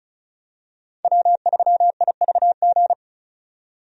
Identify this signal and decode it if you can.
Morse code, heard as a keyed tone, about 36 wpm, 700 Hz. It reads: W3IVG
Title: USA Call Signs - 5 Characters - 35wpm